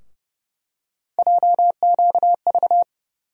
Morse code, heard as a keyed tone, about 30 wpm, 700 Hz. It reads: JQV
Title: USA Call Sign Postfixes - 3 Letters 30wpm